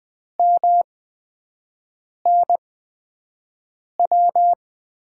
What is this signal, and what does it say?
Morse code, heard as a keyed tone, about 20 wpm, spaced as for 5 wpm, 700 Hz. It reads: MNW
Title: USA Call Signs - 6 Characters - 8x Character Spacing - 20wpm